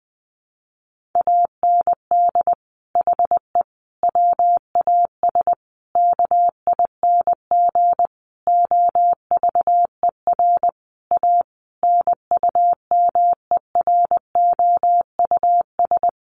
Morse code, heard as a keyed tone, about 19 wpm, 700 Hz. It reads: AND HE WAS KING OVER A NUMEROUS
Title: Sentences from Top 1400 Words 20wpm